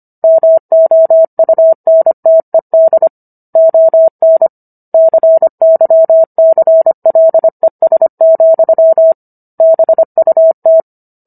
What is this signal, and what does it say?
Morse code, heard as a keyed tone, about 25 wpm, 650 Hz. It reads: MOUNTED ON CYCLES, BUT